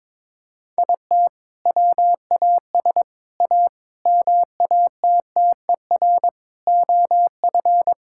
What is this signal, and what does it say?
Morse code, heard as a keyed tone, about 22 wpm, 700 Hz. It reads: IT WAS A MATTER OF
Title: Sentences from Top 1000 Words - Review Part 5 - 22wpm